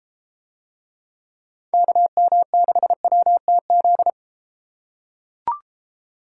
Morse code, heard as a keyed tone, about 32 wpm, 700 Hz. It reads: KM6WTZ